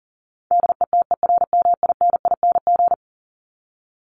Morse code, heard as a keyed tone, about 40 wpm, 700 Hz. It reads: DETERMINING
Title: ING Suffix - Top 100 Words with Base Words 40wpm